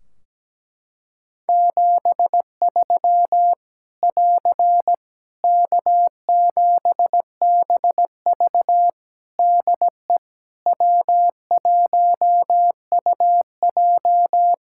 Morse code, heard as a keyed tone, about 17 wpm, 700 Hz. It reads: 73 <AR> K7BV DE W1UJ
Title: K1USN SST QSO Practice 17wpm